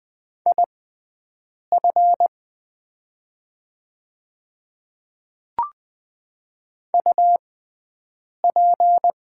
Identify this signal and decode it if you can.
Morse code, heard as a keyed tone, about 20 wpm, spaced as for 7 wpm, 700 Hz. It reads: IF UP
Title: 6x Character Spacing - Top 100 Words - 20wpm